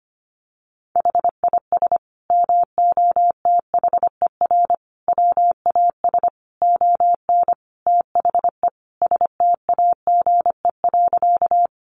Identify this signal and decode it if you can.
Morse code, heard as a keyed tone, about 25 wpm, 700 Hz. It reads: HIS MOTHER WAS ON THE STAGE.
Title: Sentences from Top 1200 Words 25wpm